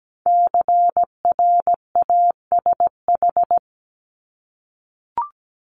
Morse code, heard as a keyed tone, about 17 wpm, 700 Hz. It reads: CRASH
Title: Multi-letter Phonemes 17wpm